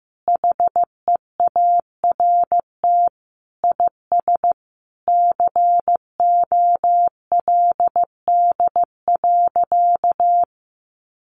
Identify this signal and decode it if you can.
Morse code, heard as a keyed tone, about 15 wpm, 700 Hz. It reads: HEART IS COLD.
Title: Sentences from Top 1200 Words 15wpm